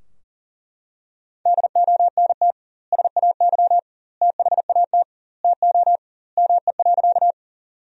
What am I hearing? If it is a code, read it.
Morse code, heard as a keyed tone, about 40 wpm, 700 Hz. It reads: DONT SAY THAT TO ME.
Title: 2x Word-Spacing - Sentences from Top 300 Words - 40wpm